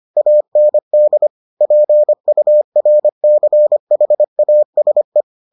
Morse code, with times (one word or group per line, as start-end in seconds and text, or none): 0.2-1.3 AND
1.6-5.2 PURCHASE